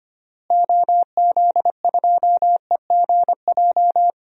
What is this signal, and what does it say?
Morse code, heard as a keyed tone, about 25 wpm, 700 Hz. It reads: OZ2EGJ
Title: DXCC-100 Call Signs - 6 Characters - Repeated 2x - 25wpm